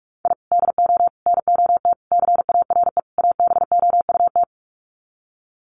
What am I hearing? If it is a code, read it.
Morse code, heard as a keyed tone, about 44 wpm, 700 Hz. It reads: I DO NOT CARE ABOUT
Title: Sets of 5 Words Encore2 45wpm